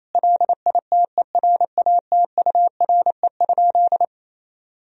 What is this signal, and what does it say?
Morse code, heard as a keyed tone, about 28 wpm, 700 Hz. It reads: LITERATURE?
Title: Sentences from Top 1000 Words - Review Part 3 - 28wpm